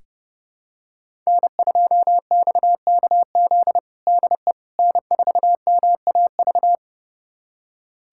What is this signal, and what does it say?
Morse code, heard as a keyed tone, about 32 wpm, 700 Hz. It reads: N2XKZ DE N4MAV
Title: USA Callsign DE Callsign - 5 Characters - 30wpm